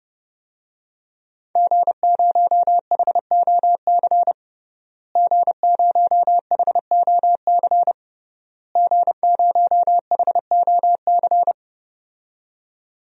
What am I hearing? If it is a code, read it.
Morse code, heard as a keyed tone, about 30 wpm, 700 Hz. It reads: G0HOC G0HOC G0HOC
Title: DXCC-100 Call Signs - 5 Characters - Repeated 3x - 30wpm